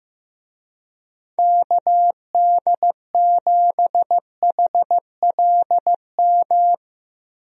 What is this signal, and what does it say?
Morse code, heard as a keyed tone, about 15 wpm, 700 Hz. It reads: KD7HLM
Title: USA Call Signs - 6 Characters - 15wpm